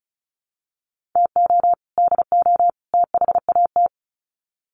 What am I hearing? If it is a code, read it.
Morse code, heard as a keyed tone, about 34 wpm, 700 Hz. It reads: TO DO THAT
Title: Sets of 3 Words 35wpm